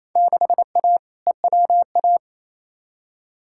Morse code, heard as a keyed tone, about 28 wpm, 700 Hz. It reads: 6A EWA
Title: Field Day Exchanges Encore - 28wpm